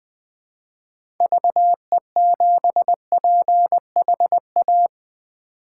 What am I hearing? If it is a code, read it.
Morse code, heard as a keyed tone, about 20 wpm, 700 Hz. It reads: VE7PHA
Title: K1USN SST QSO Practice - 20wpm